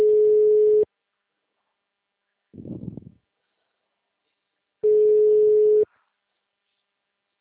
Ringback tone (Europe, 425 Hz).